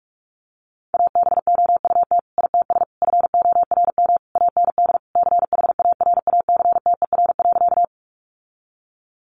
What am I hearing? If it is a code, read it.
Morse code, heard as a keyed tone, about 44 wpm, 700 Hz. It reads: ABOUT ITS FORM AND CHARACTER.
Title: Sentences from Top 1200 Words 45wpm